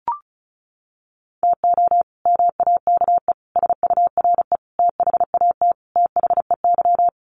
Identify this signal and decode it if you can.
Morse code, heard as a keyed tone, about 34 wpm, 700 Hz. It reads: TO MAKE SURE THAT THEY
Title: Sets of 5 Words Encore2 35wpm